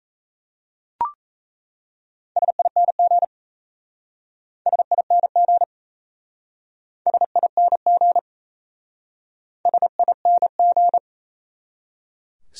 Morse code, heard as a keyed tone, about 37 wpm, 700 Hz. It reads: SING SING SING SING